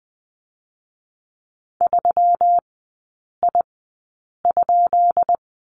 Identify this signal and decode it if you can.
Morse code, heard as a keyed tone, about 20 wpm, spaced as for 8 wpm, 700 Hz. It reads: 3I?
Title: 178 Lesson 29e - Slash - Three Characters 20wpm